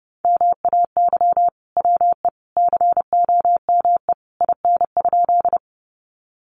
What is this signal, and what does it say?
Morse code, heard as a keyed tone, about 30 wpm, 700 Hz. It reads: MAY WE COME IN?
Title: Sentences from Top 100 Words 30wpm